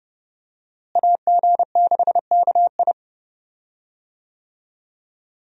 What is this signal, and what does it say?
Morse code, heard as a keyed tone, about 30 wpm, 700 Hz. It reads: AG6KI
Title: USA Call Signs - 5 Characters - Repeated 3x - 30wpm